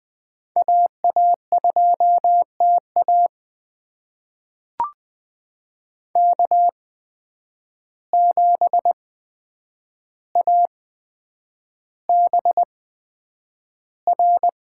Morse code, heard as a keyed tone, about 20 wpm, spaced as for 9 wpm, 700 Hz. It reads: AA2TA K7ABR
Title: USA Call Signs - 5 Characters - 8x Character Spacing - 20wpm